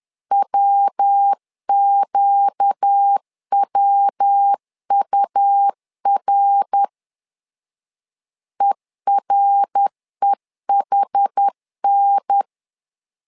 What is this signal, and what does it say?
Morse code, heard as a keyed tone, about 10 wpm, 800 Hz. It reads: WQWUR EREHN
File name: LEZIONI / 4 Q 10wpm 2210